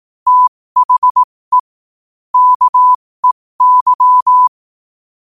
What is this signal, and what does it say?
Morse code, heard as a keyed tone, about 18 wpm, spaced as for 15 wpm, 1000 Hz. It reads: THE KEY
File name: qotdfiles / 15wpm